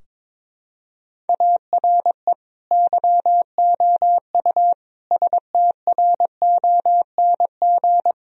Morse code, heard as a keyed tone, about 22 wpm, 700 Hz. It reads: ARE YOU STRONG
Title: Sentences from Top 1000 Words - Review Part 3 - 22wpm